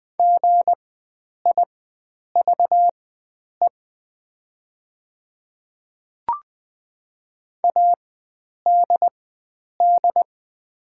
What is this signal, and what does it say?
Morse code, heard as a keyed tone, about 20 wpm, spaced as for 9 wpm, 700 Hz. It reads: GIVE ADD
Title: 4x Character Spacing - Top 300 Words - 20wpm